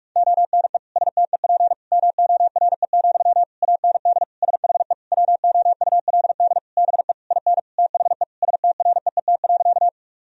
Morse code, heard as a keyed tone, about 46 wpm, 700 Hz. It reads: ONE STEP MORE, AND SHE WOULD BE IN THE STREET.
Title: Sentences from Top 1000 Words - Review Part 3 - 45wpm